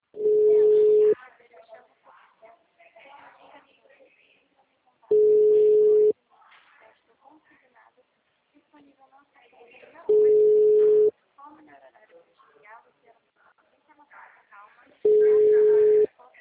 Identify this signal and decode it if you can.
Ringback tone (Europe, 425 Hz), with speech.